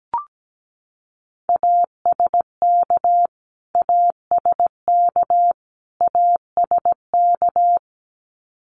Morse code, heard as a keyed tone, about 17 wpm, 700 Hz. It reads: ASK ASK ASK
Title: Top 500 Words Repeated 17wpm